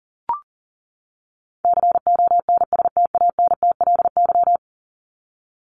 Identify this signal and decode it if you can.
Morse code, heard as a keyed tone, about 40 wpm, 700 Hz. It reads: CONSTANTLY
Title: LY Suffix - Top 100 Words with Base Words 40wpm